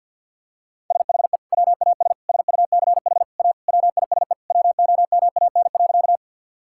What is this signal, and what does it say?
Morse code, heard as a keyed tone, about 50 wpm, 700 Hz. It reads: SHE WAS SUCH A WISE WOMAN.
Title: Sentences from Top 1600 Words 50wpm